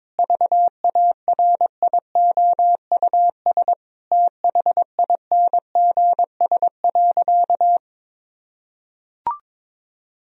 Morse code, heard as a keyed tone, about 22 wpm, 700 Hz. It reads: VARIOUS THINGS.
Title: Sentences from Top 1000 Words - Review Part 4 - 22wpm